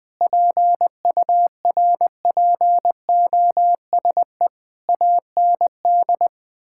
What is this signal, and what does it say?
Morse code, heard as a keyed tone, about 21 wpm, 700 Hz. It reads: PURPOSE AND
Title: Sentences with 9-Word Phrases - 20wpm